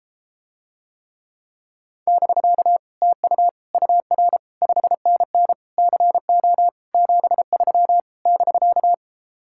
Morse code, heard as a keyed tone, about 33 wpm, 700 Hz. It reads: <BK> TU UR 5NN CO 73 <BK>